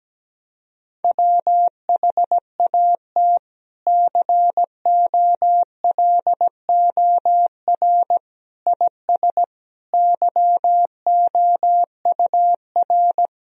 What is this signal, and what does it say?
Morse code, heard as a keyed tone, about 17 wpm, 700 Hz. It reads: WHAT COLOR IS YOUR
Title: Sentences from Top 1000 Words - Review Part 3 - 17wpm